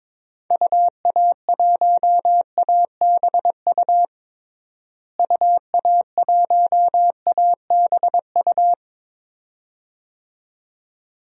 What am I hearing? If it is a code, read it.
Morse code, heard as a keyed tone, about 22 wpm, 700 Hz. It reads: UA1ABU UA1ABU
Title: DXCC-100 Call Signs - 6 Characters - Repeated 2x - 22wpm